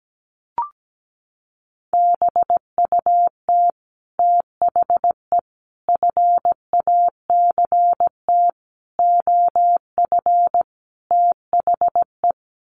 Morse code, heard as a keyed tone, about 17 wpm, 700 Hz. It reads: BUT THE FACT OF THE
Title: Sets of 5 Words Encore2 17wpm